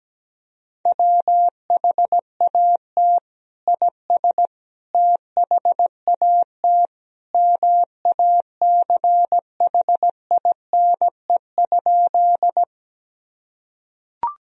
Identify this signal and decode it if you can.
Morse code, heard as a keyed tone, about 17 wpm, 700 Hz. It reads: WHAT IS THAT MACHINE?
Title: Sentences from Top 1000 Words - Review Part 2 - 17wpm